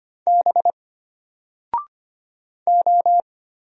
Morse code, heard as a keyed tone, about 25 wpm, 700 Hz. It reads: B O
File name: Single-Letter-Number-25wpm